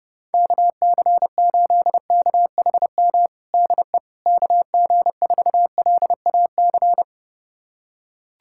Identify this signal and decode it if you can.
Morse code, heard as a keyed tone, about 30 wpm, 700 Hz. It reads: KC8KHM DE KG4LAC
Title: USA Callsign DE Callsign - 6 Characters - 30wpm